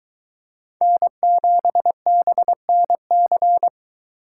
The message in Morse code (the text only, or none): N7BNC